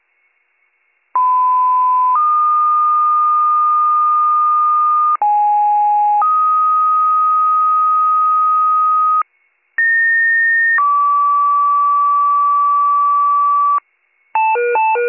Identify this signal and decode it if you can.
Two-tone page, heard as a one-second tone, then a three-second tone, about 1010 then 1220 Hz, about 830 then 1250 Hz, about 1850 then 1150 Hz. It is heard from 1.1 to 13.8 s.